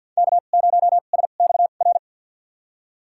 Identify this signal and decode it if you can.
Morse code, heard as a keyed tone, about 49 wpm, spaced as for 36 wpm, 700 Hz. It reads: K0SXR
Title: USA Call Signs - 5 Characters - 2x Character Spacing - 50wpm